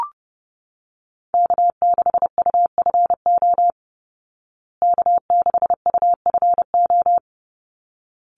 Morse code, heard as a keyed tone, about 30 wpm, 700 Hz. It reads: K6UFO K6UFO
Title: Call Signs Repeated 3x - 30wpm